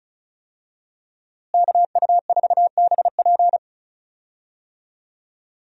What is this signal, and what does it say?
Morse code, heard as a keyed tone, about 36 wpm, 700 Hz. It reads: KU4BP
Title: USA Call Signs - 5 Characters - Repeated 2x - 35wpm